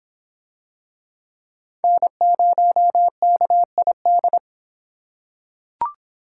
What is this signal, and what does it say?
Morse code, heard as a keyed tone, about 26 wpm, 700 Hz. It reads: N0KID